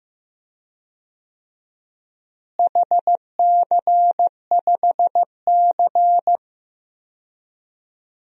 Morse code, heard as a keyed tone, about 15 wpm, 700 Hz. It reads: HC5C
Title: DXCC-100 Call Signs - 4 Characters - Repeated 3x - 15wpm